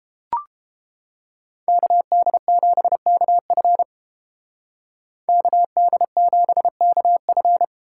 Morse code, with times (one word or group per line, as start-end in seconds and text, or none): 1.7-3.8 KD7KF
5.3-7.6 KD7KF